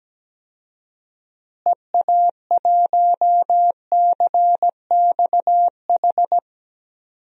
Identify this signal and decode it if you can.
Morse code, heard as a keyed tone, about 17 wpm, 700 Hz. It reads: EA1CXH
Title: DXCC-100 Call Signs - 6 Characters - Repeated 3x - 17wpm